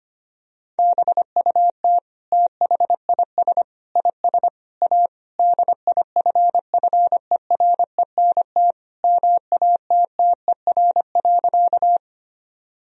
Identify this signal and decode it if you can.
Morse code, heard as a keyed tone, about 25 wpm, 700 Hz. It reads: BUT THIS IS A DIFFERENT MATTER.
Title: Sentences from Top 1000 Words - Review Part 3 - 25wpm